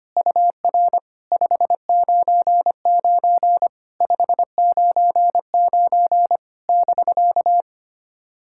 Morse code, heard as a keyed tone, about 25 wpm, 700 Hz. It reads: UR 599 599 <BK>
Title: SOTA QSO Practice 25wpm